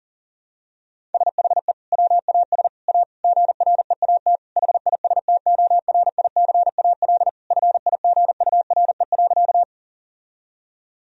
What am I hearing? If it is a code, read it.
Morse code, heard as a keyed tone, about 40 wpm, 700 Hz. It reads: SHE WAS A GREAT HISTORICAL FIGURE.
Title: Sentences from Top 1000 Words - Review Part 4 - 40wpm